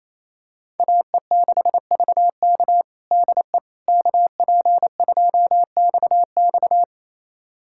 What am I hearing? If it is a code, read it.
Morse code, heard as a keyed tone, about 28 wpm, 700 Hz. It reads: AE6VK DE KP2XX
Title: USA Callsign DE Callsign - 5 Characters - 28wpm